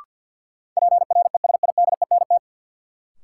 Morse code, heard as a keyed tone, about 53 wpm, 700 Hz. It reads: PRESIDENT
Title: Sentences from Top 1000 Words - New Word Drill - 50wpm